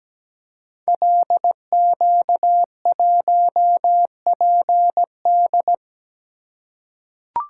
Morse code, heard as a keyed tone, about 17 wpm, 700 Hz. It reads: LQ1PD
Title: Code Groups - 5 Letter:Numbers - 17wpm